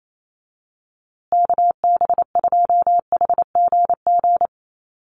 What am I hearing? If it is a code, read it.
Morse code, heard as a keyed tone, about 28 wpm, 700 Hz. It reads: KB2HGG
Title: USA Call Signs Repeated 2x 28wpm